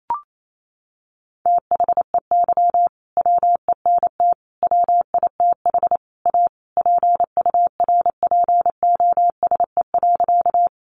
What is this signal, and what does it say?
Morse code, heard as a keyed tone, about 28 wpm, 700 Hz. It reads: THEY WENT WITH A PURPOSE.
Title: Sentences from Top 800 Words 28wpm